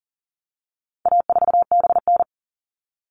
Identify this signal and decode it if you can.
Morse code, heard as a keyed tone, about 40 wpm, 700 Hz. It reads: A4BN
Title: All Cut Numbers - 4-Digit 40wpm